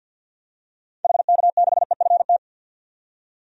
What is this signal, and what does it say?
Morse code, heard as a keyed tone, about 52 wpm, 700 Hz. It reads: HK6EFT
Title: DXCC-100 Call Signs - 6 Characters - 50wpm